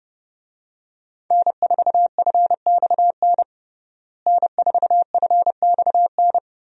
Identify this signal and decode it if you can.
Morse code, heard as a keyed tone, about 30 wpm, 700 Hz. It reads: N4FXN N4FXN
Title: USA Call Signs - 5 Characters - Repeated 2x - 30wpm